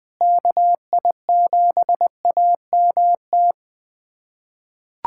Morse code, heard as a keyed tone, about 20 wpm, 700 Hz. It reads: KI7AMT
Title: USA Callsign DE Callsign - 6 Characters - 20wpm